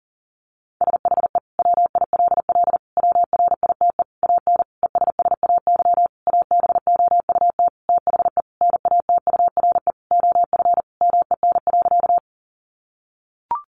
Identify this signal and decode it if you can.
Morse code, heard as a keyed tone, about 40 wpm, 700 Hz. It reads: SHE WILL WRITE AN ESSAY ABOUT THE NATURE OF MEN.
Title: Sentences from Top 1500 Words 40wpm